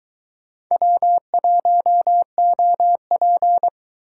Morse code, heard as a keyed tone, about 22 wpm, 700 Hz. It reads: W1OP